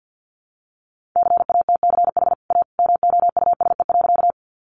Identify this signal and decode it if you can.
Morse code, heard as a keyed tone, about 50 wpm, 700 Hz. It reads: CATCH A MOUSE.
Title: Sentences from Top 1500 Words 50wpm